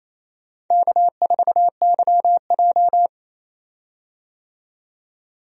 Morse code, heard as a keyed tone, about 27 wpm, 700 Hz. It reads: K4YJ
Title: USA Call Signs - 4 Characters - 28wpm